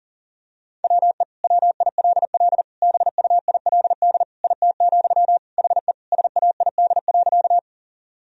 Morse code, heard as a keyed tone, about 40 wpm, 700 Hz. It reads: WE WILL BUILD IT, HE SAID.
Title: Sentences from Top 1000 Words - Review Part 1 - 40wpm